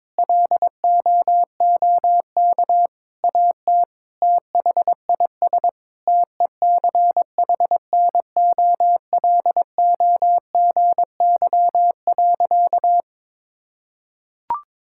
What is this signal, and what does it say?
Morse code, heard as a keyed tone, about 22 wpm, 700 Hz. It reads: LOOK AT THIS TECHNOLOGY.
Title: Sentences from Top 1000 Words - Review Part 5 - 22wpm